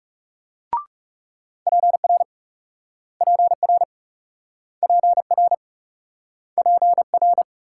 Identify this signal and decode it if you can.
Morse code, heard as a keyed tone, about 38 wpm, 700 Hz. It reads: PR PR PR PR